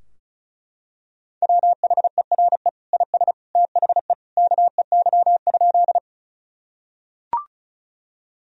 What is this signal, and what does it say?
Morse code, heard as a keyed tone, about 36 wpm, 700 Hz. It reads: WHERE IS THE KEY?
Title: Sentences from Top 1000 Words - Review Part 4 - 35wpm